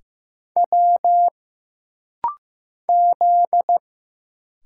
Morse code, heard as a keyed tone, about 15 wpm, 700 Hz. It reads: W Z
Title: Single Letter-Number Alt - Rapid-Fire 15wpm